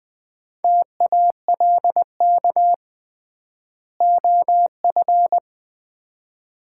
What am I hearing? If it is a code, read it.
Morse code, heard as a keyed tone, about 20 wpm, 700 Hz. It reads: TALK OF
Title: 3x Word-Spacing - Sets of 5 Words - 20wpm